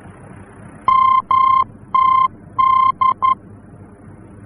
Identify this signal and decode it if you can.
Morse code, heard as a keyed tone, about 11 wpm, 1050 Hz. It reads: MTD
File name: france / mx / MTD-F-VOR